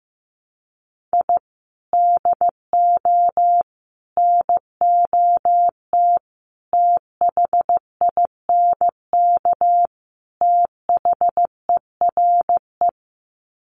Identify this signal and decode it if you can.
Morse code, heard as a keyed tone, about 15 wpm, 700 Hz. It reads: I DO NOT THINK THERE
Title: Sets of 5 Words 15wpm